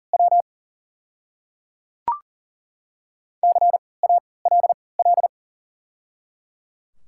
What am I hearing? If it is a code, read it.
Morse code, heard as a keyed tone, about 38 wpm, spaced as for 22 wpm, 690 Hz. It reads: W CALL
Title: 3x Character Spacing - Top 100 Words - 40wpm